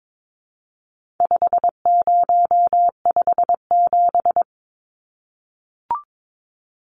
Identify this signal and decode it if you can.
Morse code, heard as a keyed tone, about 22 wpm, 700 Hz. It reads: 5057
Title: Numbers - 4-Digit 22wpm